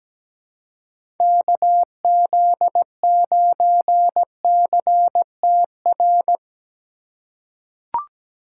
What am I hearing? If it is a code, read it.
Morse code, heard as a keyed tone, about 17 wpm, 700 Hz. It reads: KZ9CTR
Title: USA Call Signs - 6 Characters - 3x Character Spacing - 17wpm